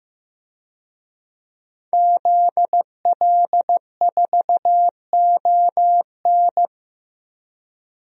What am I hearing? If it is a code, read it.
Morse code, heard as a keyed tone, about 15 wpm, 700 Hz. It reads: ZL4ON
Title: DXCC-100 Call Signs - 5 Characters - Repeated 3x - 15wpm